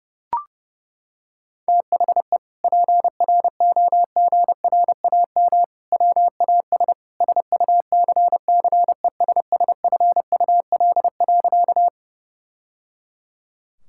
Morse code, heard as a keyed tone, about 30 wpm, 700 Hz. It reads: THE PROGRAM WAS SUCCESSFUL.
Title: Sentences from Top 1000 Words - Review Part 5 - 30wpm